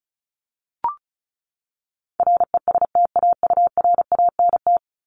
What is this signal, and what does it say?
Morse code, heard as a keyed tone, about 34 wpm, 700 Hz. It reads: RESTAURANT
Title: Top 1751-2000 Words 35wpm